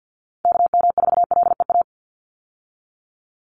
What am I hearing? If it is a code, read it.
Morse code, heard as a keyed tone, about 50 wpm, 700 Hz. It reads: KM4LEA
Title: USA Call Signs 50wpm